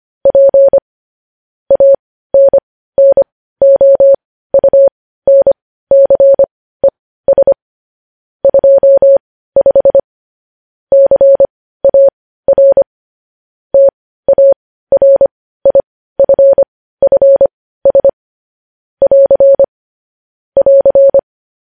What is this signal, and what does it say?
Morse code, heard as a keyed tone, about 25 wpm, spaced as for 15 wpm, 550 Hz. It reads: P ANNOUNCES 25 CAR TARIFFS <AR> <AR>
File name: podcast / News headlines-1x-25x15